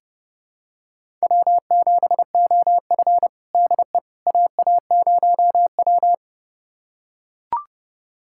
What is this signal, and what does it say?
Morse code, heard as a keyed tone, about 30 wpm, 700 Hz. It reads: W7OF DE AA0W
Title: USA Callsign DE Callsign - 4 Characters - 30wpm